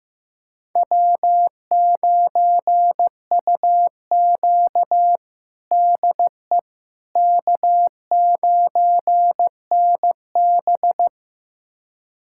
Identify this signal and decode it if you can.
Morse code, heard as a keyed tone, about 15 wpm, 700 Hz. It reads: W9UQ DE K9NB
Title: USA Callsign DE Callsign - 4 Characters - 15wpm